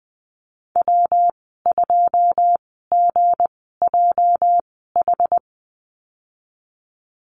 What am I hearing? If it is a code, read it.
Morse code, heard as a keyed tone, about 20 wpm, spaced as for 15 wpm, 700 Hz. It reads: W2GJH
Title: 207 Lesson 33g - Letter j - Call Signs 20wpm